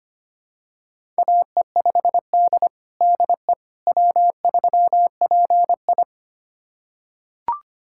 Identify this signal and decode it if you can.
Morse code, heard as a keyed tone, about 25 wpm, 700 Hz. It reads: AE5D DE W3PI
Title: USA Callsign DE Callsign - 4 Characters - 25wpm